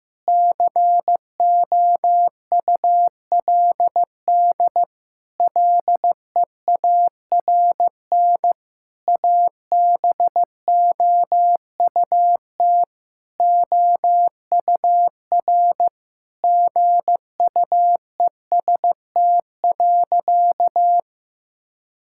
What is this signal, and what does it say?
Morse code, heard as a keyed tone, about 15 wpm, 700 Hz. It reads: COULD LEARN ABOUT OUR GUEST.
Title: Sentences from Top 1600 Words 15wpm